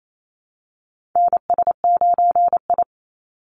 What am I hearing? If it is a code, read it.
Morse code, heard as a keyed tone, about 28 wpm, 700 Hz. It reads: NS9I
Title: Call Signs Repeated 3x - 28wpm